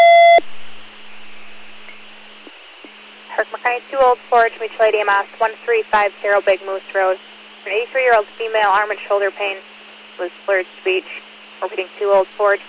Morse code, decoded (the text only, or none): none